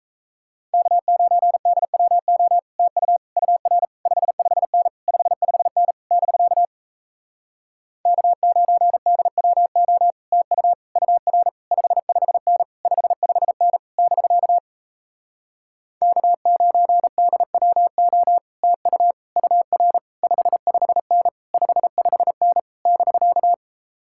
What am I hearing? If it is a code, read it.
Morse code, heard as a keyed tone, about 38 wpm, 700 Hz. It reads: K9DWO TU UR 55N 55N <BK> K9DWO TU UR 55N 55N <BK> K9DWO TU UR 55N 55N <BK>